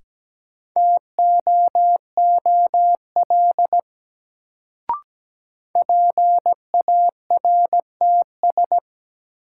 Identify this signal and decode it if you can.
Morse code, heard as a keyed tone, about 17 wpm, 700 Hz. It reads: TOOL PARTS
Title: Sentences from Top 1000 Words - Review Part 6 - New Word Drill - 17wpm